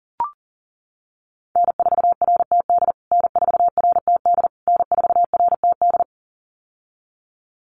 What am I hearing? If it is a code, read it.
Morse code, heard as a keyed tone, about 40 wpm, 700 Hz. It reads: N4RTD N4RTD N4RTD
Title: USA Call Signs Repeated 3x 40wpm